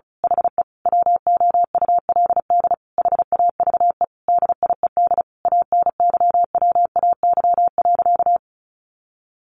Morse code, heard as a keyed tone, about 34 wpm, 700 Hz. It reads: HE WOULD HAVE DIED ANYWAY.
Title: A Sound Of Thunder 35wpm